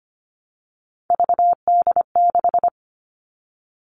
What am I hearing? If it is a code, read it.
Morse code, heard as a keyed tone, about 25 wpm, 700 Hz. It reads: VD6
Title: Three Challenging Letter-Number Combinations 25wpm